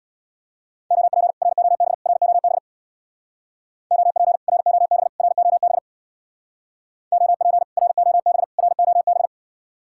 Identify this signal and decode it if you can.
Morse code, heard as a keyed tone, about 63 wpm, 700 Hz. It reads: OP ROB ROB OP ROB ROB OP ROB ROB